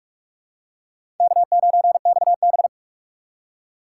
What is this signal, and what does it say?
Morse code, heard as a keyed tone, about 44 wpm, 700 Hz. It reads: K9XB
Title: USA Call Signs - 4 Characters - 6x Character Spacing - 45wpm